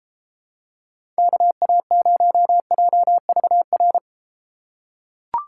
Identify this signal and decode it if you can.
Morse code, heard as a keyed tone, about 31 wpm, 700 Hz. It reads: KA0JVR